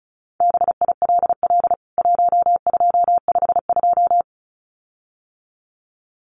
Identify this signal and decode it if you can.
Morse code, heard as a keyed tone, about 34 wpm, 700 Hz. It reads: BILL 1252
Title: CWT Exchange 35wpm